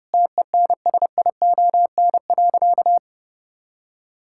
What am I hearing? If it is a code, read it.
Morse code, heard as a keyed tone, about 30 wpm, 700 Hz. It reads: TENSION.
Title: Sentences from Top 1600 Words 30wpm